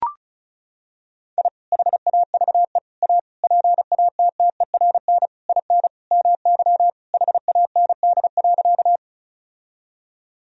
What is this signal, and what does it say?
Morse code, heard as a keyed tone, about 34 wpm, 700 Hz. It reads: I HAVE A PATTERN IN MY HAND.
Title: Sentences from Top 1000 Words - Review Part 2 - 35wpm